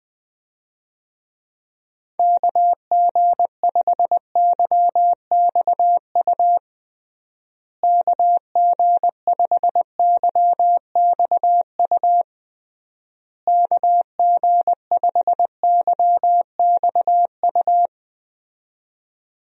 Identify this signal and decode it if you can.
Morse code, heard as a keyed tone, about 19 wpm, 700 Hz. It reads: KG5YXU KG5YXU KG5YXU
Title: USA Call Signs - 6 Characters - Repeated 3x - 20wpm